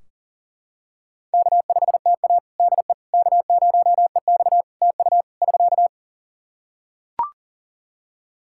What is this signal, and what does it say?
Morse code, heard as a keyed tone, about 40 wpm, 700 Hz. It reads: K5TA DE K0EX TU <SK>
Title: Practice QSO - Exchange 4 - Ending - 40wpm